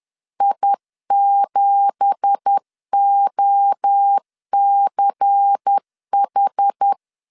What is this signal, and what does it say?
Morse code, heard as a keyed tone, about 10 wpm, 800 Hz. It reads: I7OCH
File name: CW / DET / 1 9 1500C S B 10wpm 4439